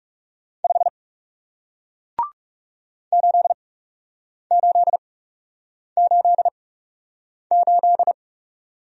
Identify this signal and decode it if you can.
Morse code, heard as a keyed tone, about 40 wpm, spaced as for 8 wpm, 700 Hz. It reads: 5 8888